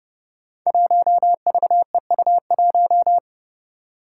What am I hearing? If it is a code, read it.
Morse code, heard as a keyed tone, about 31 wpm, 700 Hz. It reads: 1VEU1
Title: Five Challenging Letter-Number Combinations 30wpm